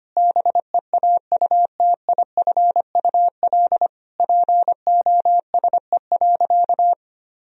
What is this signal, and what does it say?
Morse code, heard as a keyed tone, about 25 wpm, 700 Hz. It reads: BEAUTIFUL POSE.
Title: Sentences from Top 1000 Words - Review Part 2 - 25wpm